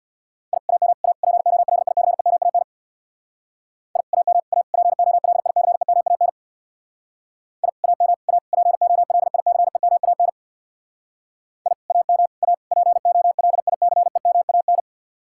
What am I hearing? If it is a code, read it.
Morse code, heard as a keyed tone, about 61 wpm, 700 Hz. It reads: I AM A POLICEMAN I AM A POLICEMAN I AM A POLICEMAN I AM A POLICEMAN
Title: Speed-Racing Common QSO Phrases - 50wpm